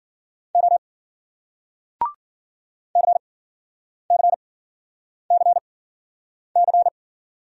Morse code, heard as a keyed tone, about 52 wpm, spaced as for 9 wpm, 700 Hz. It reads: O CCCC